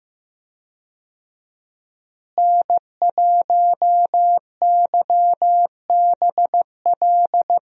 Morse code, heard as a keyed tone, about 15 wpm, 700 Hz. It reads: N1YBL
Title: USA Call Signs - 5 Characters - Repeated 2x - 15wpm